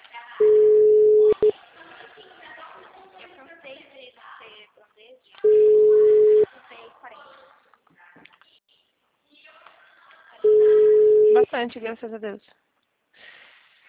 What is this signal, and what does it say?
Ringback tone (Europe, 425 Hz), with speech.